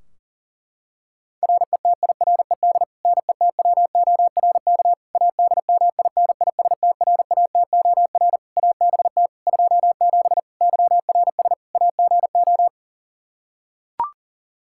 Morse code, heard as a keyed tone, about 37 wpm, 700 Hz. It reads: RETIRED NETWORK ADMINISTRATOR ABT 27 YRS AGO
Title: K1USN SST QSO Practice 40wpm